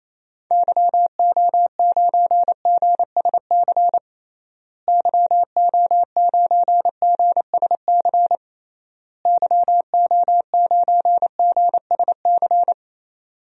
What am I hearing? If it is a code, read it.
Morse code, heard as a keyed tone, about 28 wpm, 700 Hz. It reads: YO9GSC YO9GSC YO9GSC
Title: DXCC-100 Call Signs - 6 Characters - Repeated 3x - 28wpm